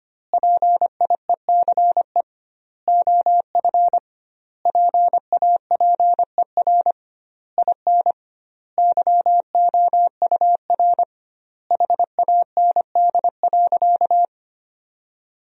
Morse code, heard as a keyed tone, about 25 wpm, 700 Hz. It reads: PIECE OF PAPER IN YOUR HAND.
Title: 2x Word-Spacing - Sentences from Top 400 Words - 25wpm